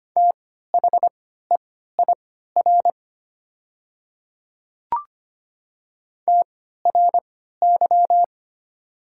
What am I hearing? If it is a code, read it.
Morse code, heard as a keyed tone, about 25 wpm, spaced as for 14 wpm, 700 Hz. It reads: THEIR TRY
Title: 3x Character Spacing - Top 200 Words - 25wpm